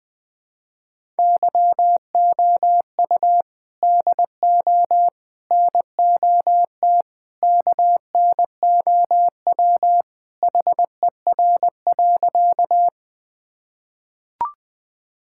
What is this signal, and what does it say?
Morse code, heard as a keyed tone, about 21 wpm, 700 Hz. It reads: YOU DO NOT KNOW HER.
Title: Sentences with 4-Word Phrases - Spoken Beginning - 20wpm